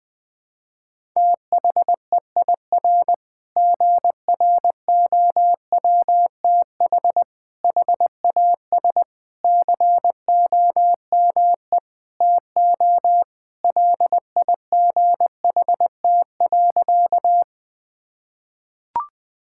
Morse code, heard as a keyed tone, about 19 wpm, 700 Hz. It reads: THEIR GROWTH HAS COME TO LIGHT.
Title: Sentences from Top 1000 Words - Review Part 5 - 20wpm